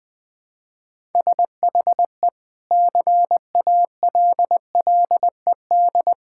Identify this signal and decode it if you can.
Morse code, heard as a keyed tone, about 20 wpm, 700 Hz. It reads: SHE CALLED
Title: Sentences with 2-Word Phrases - Spoken Beginning - 20wpm